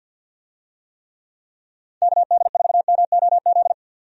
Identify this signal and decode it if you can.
Morse code, heard as a keyed tone, about 51 wpm, 700 Hz. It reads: KD4MOZ
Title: USA Call Signs - 6 Characters - 50wpm